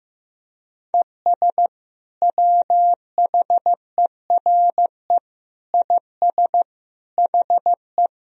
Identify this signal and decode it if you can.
Morse code, heard as a keyed tone, about 15 wpm, 700 Hz. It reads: ES WHERE IS HE
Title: Sentences from Top 200 Words and Abbreviations 15wpm